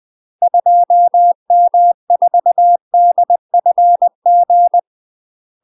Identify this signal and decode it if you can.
Morse code, heard as a keyed tone, about 20 wpm, 700 Hz. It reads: 2M4DFG
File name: UK MASTER QSOs TRAINING / 20WPM / UK MASTER QSOs 0004